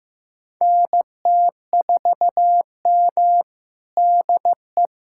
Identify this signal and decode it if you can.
Morse code, heard as a keyed tone, about 15 wpm, 700 Hz. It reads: NT4M DE
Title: USA Callsign DE Callsign - 4 Characters - 15wpm